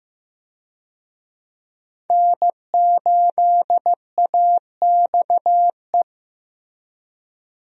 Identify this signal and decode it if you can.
Morse code, heard as a keyed tone, about 15 wpm, 700 Hz. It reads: N8AXE
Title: USA Call Signs - 5 Characters - Repeated 3x - 15wpm